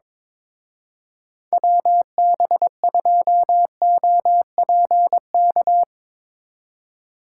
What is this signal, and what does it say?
Morse code, heard as a keyed tone, about 24 wpm, 700 Hz. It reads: WB2OPK
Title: Speed Racing - USA Call Signs 22wpm